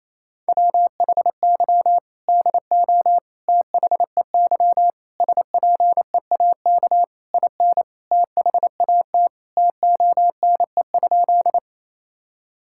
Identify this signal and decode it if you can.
Morse code, heard as a keyed tone, about 28 wpm, 700 Hz. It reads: WHY DO THEY SPEAK IN THAT TONE?
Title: Sentences from Top 1600 Words 28wpm